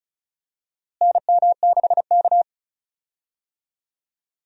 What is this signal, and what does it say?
Morse code, heard as a keyed tone, about 36 wpm, 700 Hz. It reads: NM6K
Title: USA Call Signs - 4 Characters - Repeated 2x - 35wpm